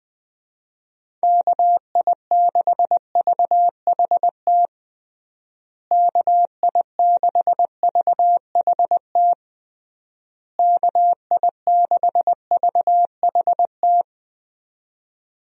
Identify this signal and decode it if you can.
Morse code, heard as a keyed tone, about 20 wpm, 700 Hz. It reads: KI6VHT KI6VHT KI6VHT
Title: USA Call Signs - 6 Characters - Repeated 3x - 20wpm